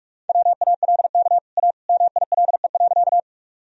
Morse code, heard as a keyed tone, about 46 wpm, 700 Hz. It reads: WALK A MILE.
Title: Sentences from Top 1000 Words - Review Part 1 - 45wpm